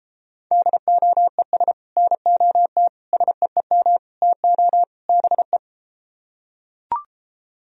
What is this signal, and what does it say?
Morse code, heard as a keyed tone, about 33 wpm, 700 Hz. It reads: DOES NOT SEEM TO BE